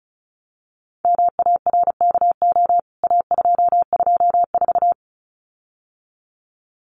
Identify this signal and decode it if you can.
Morse code, heard as a keyed tone, about 34 wpm, 700 Hz. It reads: MARKO A224
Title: CWT Exchange 35wpm